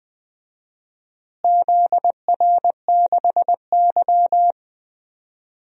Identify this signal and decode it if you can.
Morse code, heard as a keyed tone, about 21 wpm, 700 Hz. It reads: ZR6Y
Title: DXCC-100 Call Signs - 4 Characters - Repeated 3x - 20wpm